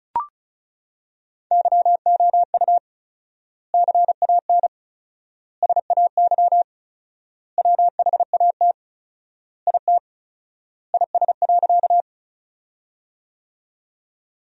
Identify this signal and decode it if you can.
Morse code, heard as a keyed tone, about 36 wpm, 700 Hz. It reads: YOU CAN SAY WHAT IT IS.
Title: 4x Word-Spacing - Sentences from Top 200 Words - 35wpm